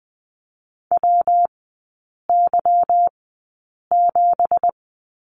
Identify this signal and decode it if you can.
Morse code, heard as a keyed tone, about 19 wpm, spaced as for 8 wpm, 690 Hz. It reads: WY7
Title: 166 Lesson 27e - Number 7 - Three Characters 20wpm